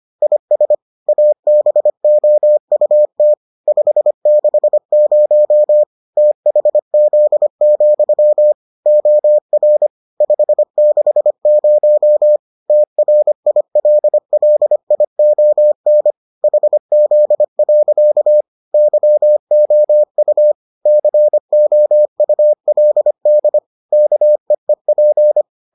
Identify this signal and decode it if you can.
Morse code, heard as a keyed tone, about 25 wpm, 600 Hz. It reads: IS ABOUT 560 THZ, OR 560 TRILLION HZ. YOU COULD KEEP